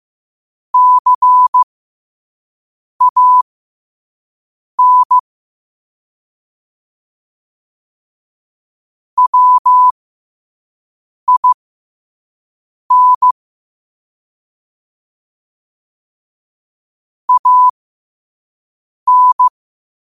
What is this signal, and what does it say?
Morse code, heard as a keyed tone, about 15 wpm, spaced as for 5 wpm, 1000 Hz. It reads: CAN WIN AN